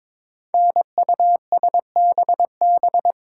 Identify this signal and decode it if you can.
Morse code, heard as a keyed tone, about 22 wpm, 700 Hz. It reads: NUSBB
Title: Code Groups - 5 Letters - 22wpm